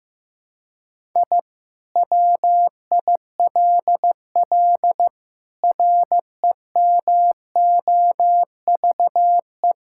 Morse code, heard as a keyed tone, about 15 wpm, 700 Hz. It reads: I WILL REMOVE
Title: Sentences from Top 1000 Words - Review Part 5 - 15wpm